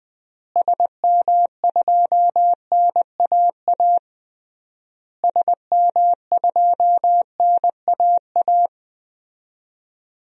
Morse code, heard as a keyed tone, about 20 wpm, 700 Hz. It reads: SM2NAA SM2NAA
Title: DXCC-100 Call Signs - 6 Characters - Repeated 3x - 20wpm